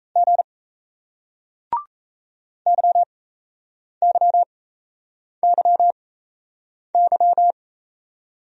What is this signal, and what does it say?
Morse code, heard as a keyed tone, about 38 wpm, spaced as for 8 wpm, 700 Hz. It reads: G YYYY